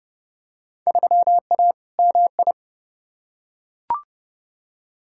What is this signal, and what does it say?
Morse code, heard as a keyed tone, about 30 wpm, 700 Hz. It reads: 3A MI
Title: Field Day Exchanges - 30wpm